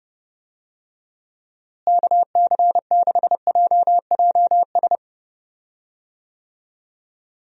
Morse code, heard as a keyed tone, about 31 wpm, 700 Hz. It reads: KC6JJS
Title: USA Call Signs - 6 Characters - Repeated 2x - 30wpm